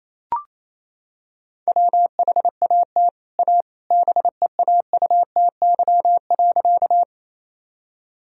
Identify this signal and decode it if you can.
Morse code, heard as a keyed tone, about 28 wpm, 700 Hz. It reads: WHAT A BEAUTY.
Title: Sentences from Top 1000 Words - Review Part 2 - 28wpm